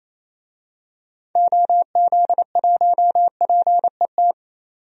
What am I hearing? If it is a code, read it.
Morse code, heard as a keyed tone, about 28 wpm, 700 Hz. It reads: OZ1PET
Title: DXCC-100 Call Signs - 6 Characters - 28wpm